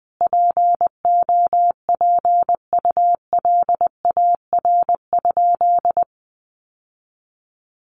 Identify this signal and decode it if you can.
Morse code, heard as a keyed tone, about 20 wpm, 700 Hz. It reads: POPULAR?
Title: Sentences from Top 1200 Words 20wpm